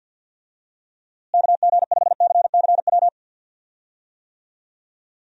Morse code, heard as a keyed tone, about 48 wpm, 700 Hz. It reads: KG5CCW
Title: USA Call Signs - 6 Characters - Repeated 3x - 50wpm